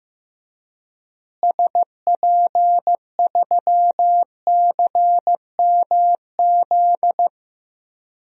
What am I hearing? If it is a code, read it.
Morse code, heard as a keyed tone, about 16 wpm, 700 Hz. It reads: SP3CMZ
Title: DXCC-100 Call Signs - 6 Characters - Repeated 2x - 15wpm